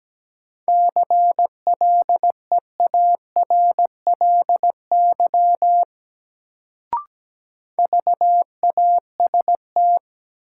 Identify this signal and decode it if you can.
Morse code, heard as a keyed tone, about 17 wpm, 700 Hz. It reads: CLEARLY VAST
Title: Sentences from Top 1300 Words - New Word Drill - 17wpm